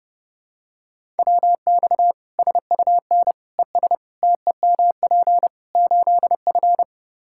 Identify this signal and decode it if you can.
Morse code, heard as a keyed tone, about 30 wpm, 700 Hz. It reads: WX SUN ES TEMP 8F
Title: Practice QSO - Exchange 2 - Info - 30wpm